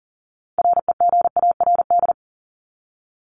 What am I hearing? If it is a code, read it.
Morse code, heard as a keyed tone, about 40 wpm, 700 Hz. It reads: REGARD
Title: ING Suffix - Top 100 Words with Base Words 40wpm